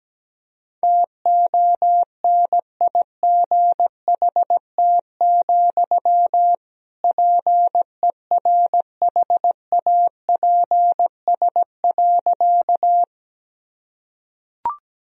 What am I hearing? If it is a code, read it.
Morse code, heard as a keyed tone, about 17 wpm, 700 Hz. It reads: TONIGHT, PERHAPS.
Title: Sentences from Top 1000 Words - Review Part 3 - 17wpm